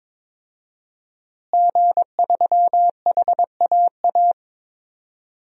Morse code, heard as a keyed tone, about 22 wpm, 700 Hz. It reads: G3HAA
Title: DXCC-100 Call Signs - 5 Characters - Repeated 2x - 22wpm